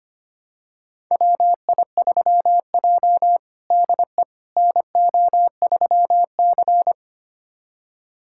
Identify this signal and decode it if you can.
Morse code, heard as a keyed tone, about 25 wpm, 700 Hz. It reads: WI3J DE NO3C
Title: USA Callsign DE Callsign - 4 Characters - 25wpm